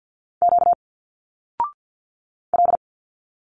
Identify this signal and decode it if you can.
Morse code, heard as a keyed tone, about 50 wpm, 700 Hz. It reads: Q L
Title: Single Letters 50wpm